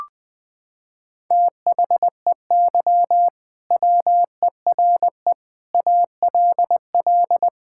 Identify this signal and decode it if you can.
Morse code, heard as a keyed tone, about 20 wpm, 700 Hz. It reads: THEY WERE ALL
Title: Sentences from Top 1000 Words - Review Part 5 - 20wpm